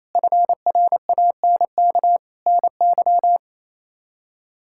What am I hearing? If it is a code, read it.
Morse code, heard as a keyed tone, about 28 wpm, 700 Hz. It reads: FRANK NY
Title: K1USN SST QSO Practice - 28wpm